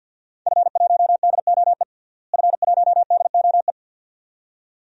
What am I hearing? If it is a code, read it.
Morse code, heard as a keyed tone, about 52 wpm, 700 Hz. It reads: F1DOE F1DOE
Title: DXCC-100 Call Signs - 5 Characters - Repeated 2x - 50wpm